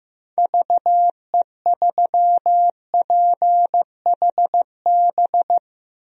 Morse code, heard as a keyed tone, about 16 wpm, 700 Hz. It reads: VE3PHB
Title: DXCC-100 Call Signs - 6 Characters - Repeated 3x - 15wpm